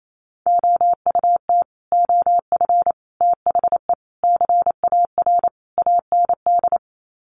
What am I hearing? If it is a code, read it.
Morse code, heard as a keyed tone, about 28 wpm, 690 Hz. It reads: OUT OF THE CAR AND
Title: Sets of 5 Words Encore 28wpm